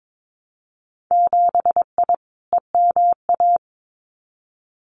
Morse code, heard as a keyed tone, about 22 wpm, 700 Hz. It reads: 7I EMA
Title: Winter Field Day Exchanges Encore 22wpm